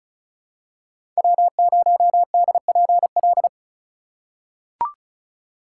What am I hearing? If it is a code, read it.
Morse code, heard as a keyed tone, about 34 wpm, 700 Hz. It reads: W0DPL
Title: USA Call Signs - 5 Characters - 3x Character Spacing - 35wpm